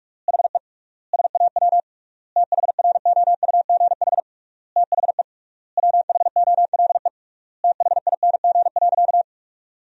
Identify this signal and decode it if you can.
Morse code, heard as a keyed tone, about 46 wpm, 700 Hz. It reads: HE SAW THROUGH THE WHOLE THING.
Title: 3x Word-Spacing - Sentences from Top 400 Words - 45wpm